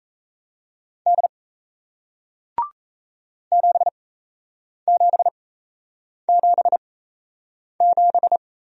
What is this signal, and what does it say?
Morse code, heard as a keyed tone, about 38 wpm, spaced as for 8 wpm, 700 Hz. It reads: D 7777